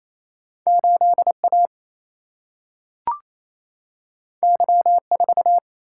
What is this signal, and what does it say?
Morse code, heard as a keyed tone, about 28 wpm, 700 Hz. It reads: 8 A Y 4
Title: Single-Letter Single-Number 28wpm